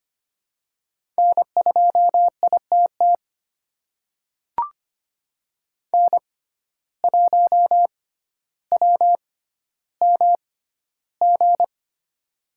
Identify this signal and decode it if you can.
Morse code, heard as a keyed tone, about 25 wpm, spaced as for 13 wpm, 700 Hz. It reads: N2ITT N1WMG
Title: USA Call Signs - 5 Characters - 6x Character Spacing - 25wpm